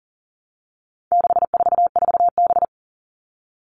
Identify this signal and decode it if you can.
Morse code, heard as a keyed tone, about 40 wpm, 700 Hz. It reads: 644B
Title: All Cut Numbers - 4-Digit 40wpm